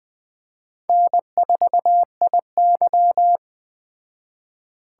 Morse code, heard as a keyed tone, about 21 wpm, 700 Hz. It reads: N4IY
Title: Speed Racing - USA 1x2 Call Signs 17wpm